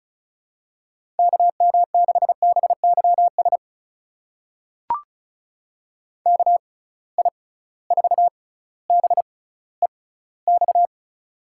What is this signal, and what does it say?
Morse code, heard as a keyed tone, about 34 wpm, spaced as for 18 wpm, 700 Hz. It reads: KM6BYS KI4BEX
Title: USA Call Signs - 6 Characters - 6x Character Spacing - 35wpm